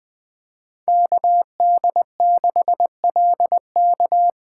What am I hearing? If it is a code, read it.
Morse code, heard as a keyed tone, about 20 wpm, 700 Hz. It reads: KD6LK
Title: USA Call Signs - 2x Character Spacing - 20wpm